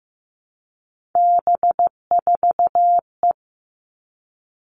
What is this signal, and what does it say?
Morse code, heard as a keyed tone, about 15 wpm, 700 Hz. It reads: B4E
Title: All Cut Numbers - 3-Digit 15wpm